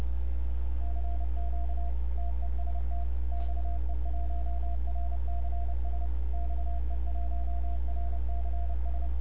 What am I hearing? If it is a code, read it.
Morse code, heard as a keyed tone, about 28 wpm, 690 Hz, with noise. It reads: CQ TEST XE1RCS XE1RCS
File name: n6tr / XE1RCS 1025UTC